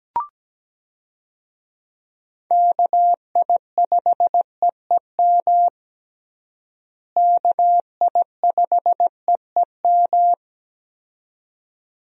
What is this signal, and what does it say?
Morse code, heard as a keyed tone, about 17 wpm, 700 Hz. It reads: KI5EEM KI5EEM
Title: USA Call Signs - 6 Characters - Repeated 2x - 17wpm